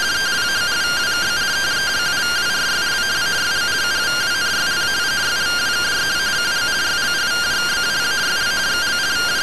Two-tone teleprinter signal, about 170 Hz shift.
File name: findings / MR AXIAL 0mm 6mm.gen-11 rtty rx